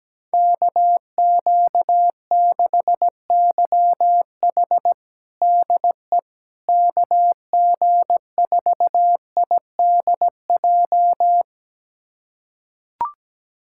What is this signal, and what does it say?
Morse code, heard as a keyed tone, about 17 wpm, 700 Hz. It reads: KQ6YH DE KG4IDJ
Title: USA Callsign DE Callsign 17wpm